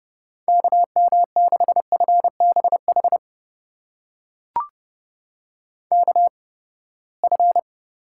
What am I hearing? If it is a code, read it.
Morse code, heard as a keyed tone, about 30 wpm, spaced as for 21 wpm, 700 Hz. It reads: KM6FBH KF
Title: USA Call Signs - 6 Characters - 8x Character Spacing - 30wpm